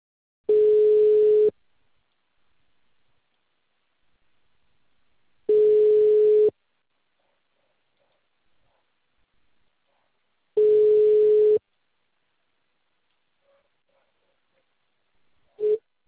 Ringback tone (Europe, 425 Hz).
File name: out- -2035- - - .85726